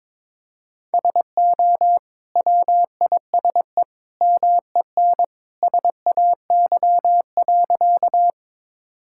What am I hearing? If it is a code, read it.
Morse code, heard as a keyed tone, about 21 wpm, 700 Hz. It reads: SO WISE MEN SAY.
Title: Sentences from Top 1600 Words 22wpm